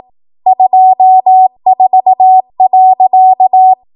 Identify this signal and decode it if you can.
Morse code, heard as a keyed tone, about 18 wpm, 750 Hz. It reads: 24.
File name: Archive / 18 WPM / 18WPM